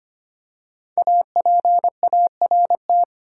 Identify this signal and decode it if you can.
Morse code, heard as a keyed tone, about 25 wpm, 700 Hz. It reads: APART
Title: Five Letter Words - 25wpm